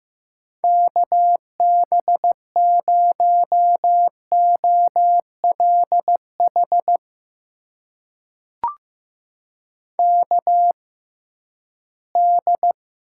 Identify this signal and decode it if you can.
Morse code, heard as a keyed tone, about 15 wpm, spaced as for 11 wpm, 700 Hz. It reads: KB0OLH KD
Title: USA Call Signs - 6 Characters - 6x Character Spacing - 15wpm